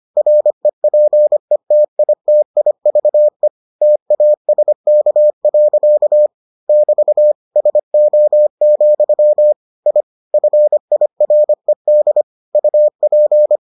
Morse code, heard as a keyed tone, about 25 wpm, 600 Hz. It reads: REPETITIVE TASK. <BT> SO, I FIRED UP